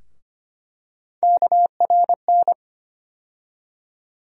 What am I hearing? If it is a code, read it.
Morse code, heard as a keyed tone, about 25 wpm, 700 Hz. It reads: KRN
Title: USA Call Sign Postfixes - 3 Letters 25wpm